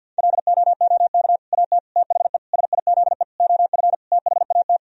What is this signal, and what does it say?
Morse code, heard as a keyed tone, about 50 wpm, 700 Hz. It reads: LOOK AT THE SIZE OF THAT
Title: Sentences from Top 1000 Words - Review Part 3 - 50wpm